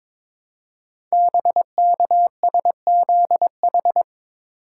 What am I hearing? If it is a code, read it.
Morse code, heard as a keyed tone, about 22 wpm, 700 Hz. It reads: BKSZH
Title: Code Groups - 5 Letters - 22wpm